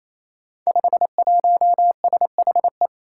Morse code, heard as a keyed tone, about 28 wpm, 700 Hz. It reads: 51SHE
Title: Five Challenging Letter-Number Combinations 28wpm